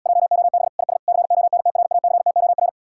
Morse code, heard as a keyed tone, about 80 wpm, 700 Hz. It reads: JOB IS CONTRACTOR